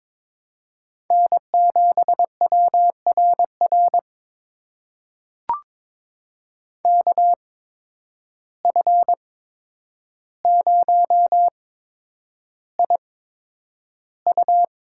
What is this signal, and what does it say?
Morse code, heard as a keyed tone, about 22 wpm, spaced as for 9 wpm, 700 Hz. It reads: N7WRR KF0IU
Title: USA Call Signs - 8x Character Spacing - 22wpm